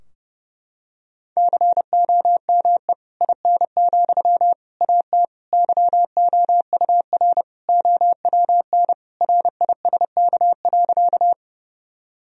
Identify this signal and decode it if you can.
Morse code, heard as a keyed tone, about 30 wpm, 700 Hz. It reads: COME IN, AT YOUR OWN RISK.
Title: Sentences from Top 1000 Words - Review Part 4 - 30wpm